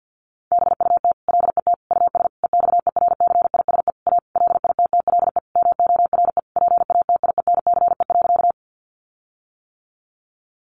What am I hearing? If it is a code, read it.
Morse code, heard as a keyed tone, about 50 wpm, 700 Hz. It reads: BUT LET US EXERCISE A LITTLE MORE PATIENCE.
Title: Sentences from Top 1300 Words 50wpm